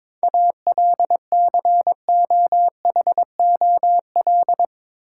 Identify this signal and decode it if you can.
Morse code, heard as a keyed tone, about 22 wpm, 700 Hz. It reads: ALCOHOL
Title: Sentences from Top 1400 Words - New Word Drill - 22wpm